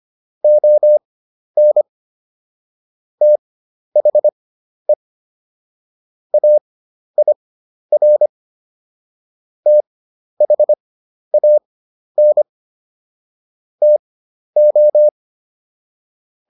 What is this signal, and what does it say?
Morse code, heard as a keyed tone, about 25 wpm, spaced as for 11 wpm, 600 Hz. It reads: ON THE AIR THAN TO